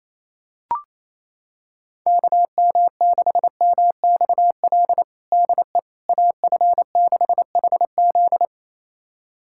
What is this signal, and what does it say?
Morse code, heard as a keyed tone, about 28 wpm, 700 Hz. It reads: KM6MXL DE AF6HZ
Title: USA Callsign DE Callsign 28wpm